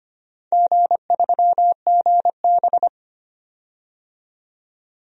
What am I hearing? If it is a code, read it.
Morse code, heard as a keyed tone, about 25 wpm, 700 Hz. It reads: G3GB
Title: DXCC-100 Call Signs - 4 Characters - 25wpm